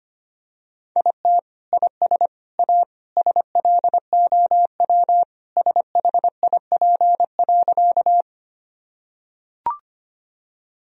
Morse code, heard as a keyed tone, about 25 wpm, 700 Hz. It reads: IT IS A SLOW SHIP.
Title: Sentences from Top 1000 Words - Review Part 2 - 25wpm